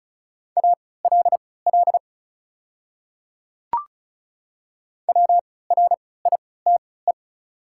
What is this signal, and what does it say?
Morse code, heard as a keyed tone, about 36 wpm, spaced as for 20 wpm, 700 Hz. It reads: ALL WRITE
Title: 3x Character Spacing - Top 200 Words - 35wpm